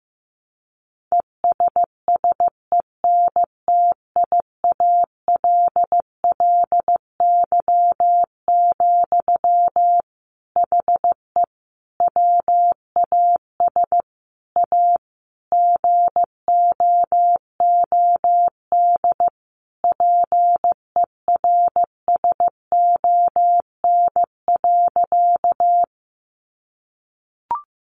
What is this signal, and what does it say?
Morse code, heard as a keyed tone, about 15 wpm, 700 Hz. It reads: ESSENTIALLY, HE WAS A GOOD PERSON.
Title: Sentences from Top 1300 Words 15wpm